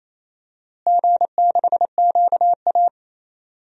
Morse code, heard as a keyed tone, about 28 wpm, 700 Hz. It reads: G6QA
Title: DXCC-100 Call Signs - 4 Characters - Repeated 2x - 28wpm